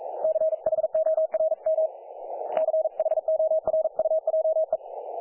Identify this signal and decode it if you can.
Morse code, heard as a keyed tone, about 40 wpm, 630 Hz, with radio noise. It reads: CHORM CHORUJE
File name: dlazaawansowanych / nagraniazpasma / websdr recording 3564.7kHz